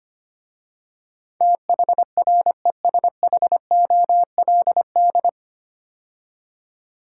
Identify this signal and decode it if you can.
Morse code, heard as a keyed tone, about 25 wpm, 700 Hz. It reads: THRESHOLD
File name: practice / 25wpm / 9 Letter Words - First 3 Letters Given - 25wpm